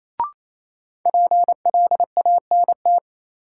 Morse code, heard as a keyed tone, about 28 wpm, 700 Hz. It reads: PLANT
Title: Sentences from Top 1000 Words - Review Part 1 - New Word Drill - 28wpm